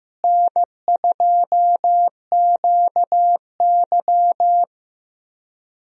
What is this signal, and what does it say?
Morse code, heard as a keyed tone, about 15 wpm, 700 Hz. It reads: N2QY
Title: USA Call Signs - 4 Characters - Repeated 2x - 15wpm